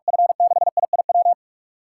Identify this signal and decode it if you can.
Morse code, heard as a keyed tone, about 46 wpm, 700 Hz. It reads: F6IIW
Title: USA Callsign DE Callsign 45wpm